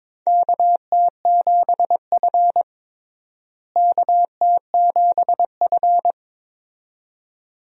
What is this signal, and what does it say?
Morse code, heard as a keyed tone, about 22 wpm, 700 Hz. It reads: KT7F KT7F
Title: USA Call Signs - 4 Characters - Repeated 2x - 22wpm